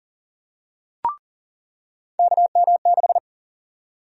Morse code, heard as a keyed tone, about 40 wpm, 700 Hz. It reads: KM6
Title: USA Call Sign Prefixes - 40wpm